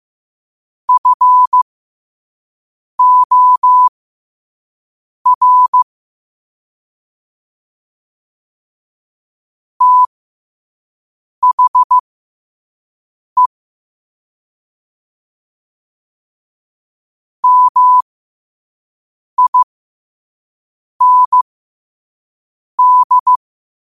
Morse code, heard as a keyed tone, about 15 wpm, spaced as for 5 wpm, 1000 Hz. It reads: FOR THE MIND
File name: qotdfiles / 5wpm